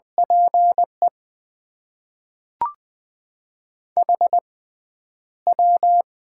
Morse code, heard as a keyed tone, about 20 wpm, spaced as for 10 wpm, 700 Hz. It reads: PE HW
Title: Top 90 QSO Elements - 6x Character Spacing - 20wpm